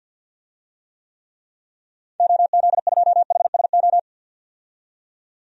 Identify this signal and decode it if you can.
Morse code, heard as a keyed tone, about 51 wpm, 700 Hz. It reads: OZ2HSO
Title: DXCC-100 Call Signs - 6 Characters - 50wpm